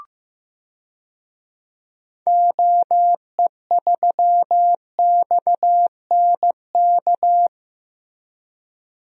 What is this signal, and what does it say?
Morse code, heard as a keyed tone, about 16 wpm, 700 Hz. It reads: OE3XNK
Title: DXCC-100 Call Signs - 6 Characters - Repeated 3x - 15wpm